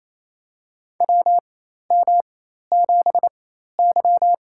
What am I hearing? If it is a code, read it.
Morse code, heard as a keyed tone, about 27 wpm, spaced as for 13 wpm, 700 Hz. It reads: WM7Y
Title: USA Call Signs - 4 Characters - 4x Character Spacing - 28wpm